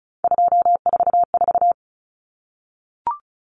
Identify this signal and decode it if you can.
Morse code, heard as a keyed tone, about 34 wpm, 700 Hz. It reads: 244
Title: Numbers - 3-Digit 35wpm